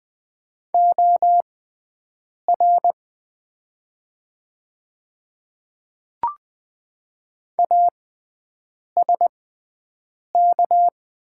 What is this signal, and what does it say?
Morse code, heard as a keyed tone, about 20 wpm, spaced as for 7 wpm, 700 Hz. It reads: OR ASK
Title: 6x Character Spacing - Top 200 Words - 20wpm